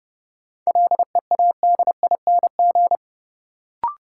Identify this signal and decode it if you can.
Morse code, heard as a keyed tone, about 30 wpm, 700 Hz. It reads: LEADING
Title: Sentences from Top 1100 Words - New Word Drill - 30wpm